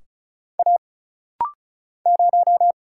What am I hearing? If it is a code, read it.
Morse code, heard as a keyed tone, about 34 wpm, 700 Hz. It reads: A 0
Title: Single Letter-Number Alt - Rapid-Fire 35wpm